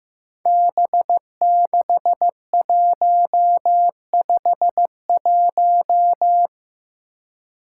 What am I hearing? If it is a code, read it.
Morse code, heard as a keyed tone, about 15 wpm, 700 Hz. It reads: B6151
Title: Five Challenging Letter-Number Combinations 15wpm